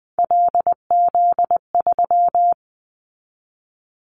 Morse code, heard as a keyed tone, about 20 wpm, 700 Hz. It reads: LZ3
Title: Callsign Prefixes for Top 100 DXCC - 20wpm